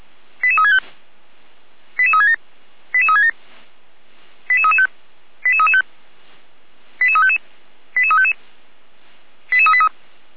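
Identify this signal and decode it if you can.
Five-tone selective call (ZVEI), heard as quick runs of five tones, tones of about 70 ms. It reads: 88365 88367 88367 88335 88335 88355 88355 88383
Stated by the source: ZVEI five-tone selective call 88073 received by SDR.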